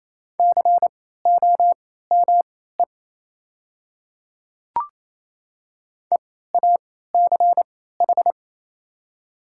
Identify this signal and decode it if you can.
Morse code, heard as a keyed tone, about 28 wpm, spaced as for 16 wpm, 700 Hz. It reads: COME EACH
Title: 3x Character Spacing - Top 200 Words - 28wpm